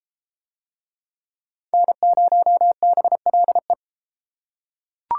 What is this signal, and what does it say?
Morse code, heard as a keyed tone, about 32 wpm, 700 Hz. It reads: N0BLE